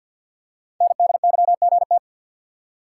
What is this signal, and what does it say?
Morse code, heard as a keyed tone, about 45 wpm, 700 Hz. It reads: NDYGT
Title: Code Groups - 5 Letter:Numbers - 50wpm